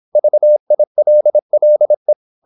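Morse code, heard as a keyed tone, about 26 wpm, 600 Hz. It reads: VILLE
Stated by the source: High-speed (QRQ) Morse at 26 wpm.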